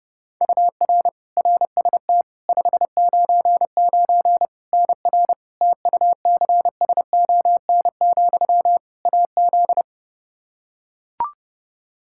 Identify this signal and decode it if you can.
Morse code, heard as a keyed tone, about 30 wpm, 700 Hz. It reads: UR RST 599 NR TUCSON, AZ
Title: Practice QSO - Exchange 1 - Beginning - 30wpm